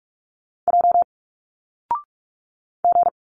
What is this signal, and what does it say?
Morse code, heard as a keyed tone, about 46 wpm, 700 Hz. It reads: J G
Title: Single Letters 45wpm